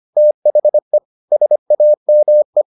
Morse code, heard as a keyed tone, about 25 wpm, 600 Hz. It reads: THE SAME